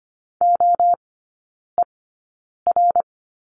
Morse code, heard as a keyed tone, about 25 wpm, spaced as for 9 wpm, 700 Hz. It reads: OER
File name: Lesson 17e - Letter u - Three Characters 25wpm